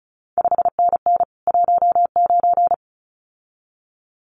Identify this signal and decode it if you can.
Morse code, heard as a keyed tone, about 36 wpm, 700 Hz. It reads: 5NN 19
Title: CQ WW DX Contest Exchange Encore - 35wpm